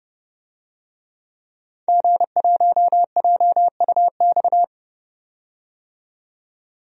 Morse code, heard as a keyed tone, about 31 wpm, 700 Hz. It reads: G1JUX
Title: DXCC-100 Call Signs - 5 Characters - 30wpm